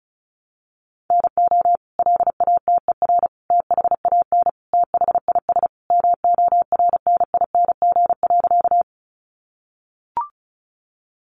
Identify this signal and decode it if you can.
Morse code, heard as a keyed tone, about 34 wpm, 700 Hz. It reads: NO LATER THAN THIS MORNING.
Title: Sentences from Top 700 Words 35wpm